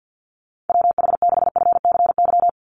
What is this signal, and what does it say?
Morse code, heard as a keyed tone, about 50 wpm, 700 Hz. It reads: WH6FCY
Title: USA Call Signs 50wpm